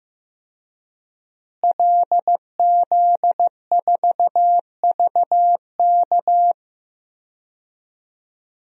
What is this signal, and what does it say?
Morse code, heard as a keyed tone, about 15 wpm, 700 Hz. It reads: LZ4VK
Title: DXCC-100 Call Signs - 5 Characters - Repeated 3x - 15wpm